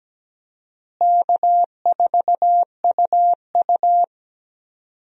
Morse code, heard as a keyed tone, about 17 wpm, 700 Hz. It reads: K4UU
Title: USA Call Signs - 4 Characters - Repeated 2x - 17wpm